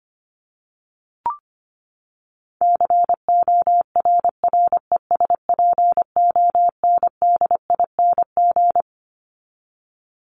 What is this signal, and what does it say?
Morse code, heard as a keyed tone, about 25 wpm, 700 Hz. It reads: CORRESPONDING
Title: ING Suffix - Top 100 Words with Base Words 25wpm